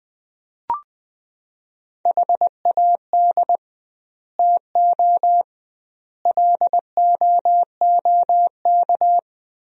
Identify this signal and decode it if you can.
Morse code, heard as a keyed tone, about 21 wpm, 700 Hz. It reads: HAD TO LOOK
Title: 2x Word-Spacing - Sentences from Top 100 Words - 20wpm